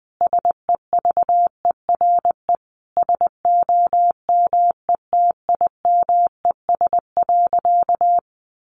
Morse code, heard as a keyed tone, about 20 wpm, 700 Hz. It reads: SEVERE SOMETIMES.
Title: Sentences from Top 1400 Words 20wpm